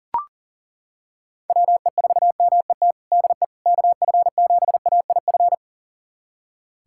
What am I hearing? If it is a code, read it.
Morse code, heard as a keyed tone, about 37 wpm, 700 Hz. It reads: WE4MET DE KF7AIF
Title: USA Callsign DE Callsign - 6 Characters - 40wpm